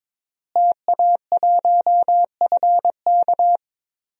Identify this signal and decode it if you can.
Morse code, heard as a keyed tone, about 21 wpm, 700 Hz. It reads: TA1FK
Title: DXCC-100 Call Signs - 5 Characters - Repeated 3x - 22wpm